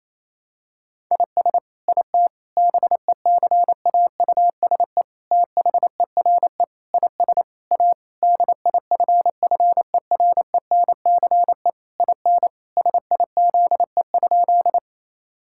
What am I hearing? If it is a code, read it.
Morse code, heard as a keyed tone, about 28 wpm, 700 Hz. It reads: IS IT BECAUSE THERE IS A DIFFERENCE IN SIZE?
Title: Sentences from Top 1000 Words - Review Part 5 - 28wpm